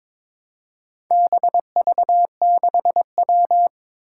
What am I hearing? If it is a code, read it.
Morse code, heard as a keyed tone, about 22 wpm, 700 Hz. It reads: BV6W
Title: Four Challenging Letter-Number Combinations 22wpm